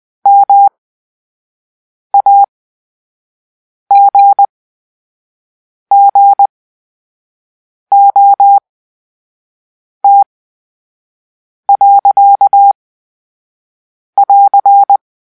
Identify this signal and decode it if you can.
Morse code, heard as a keyed tone, about 20 wpm, spaced as for 5 wpm, 800 Hz. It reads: MAGGOT.<AR>
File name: Grp-3 Drill-2